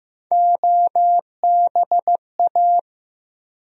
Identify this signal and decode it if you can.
Morse code, heard as a keyed tone, about 15 wpm, 700 Hz. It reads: OBA
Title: USA Call Sign Postfixes - 3 Letters 15wpm